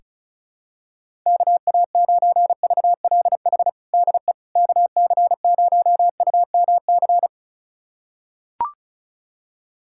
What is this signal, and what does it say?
Morse code, heard as a keyed tone, about 34 wpm, 700 Hz. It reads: KA9VLH DE KC0UMC
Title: USA Callsign DE Callsign - 6 Characters - 35wpm